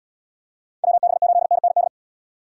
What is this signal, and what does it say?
Morse code, heard as a keyed tone, about 77 wpm, 700 Hz. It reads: WB8NND